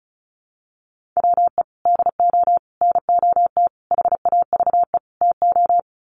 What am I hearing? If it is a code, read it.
Morse code, heard as a keyed tone, about 34 wpm, 700 Hz. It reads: WE DO NOT HAVE TO
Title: Sets of 5 Words 35wpm